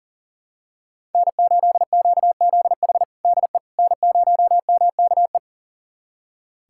Morse code, heard as a keyed tone, about 40 wpm, 700 Hz. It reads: N8QZH DE N0MKE
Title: USA Callsign DE Callsign - 5 Characters - 40wpm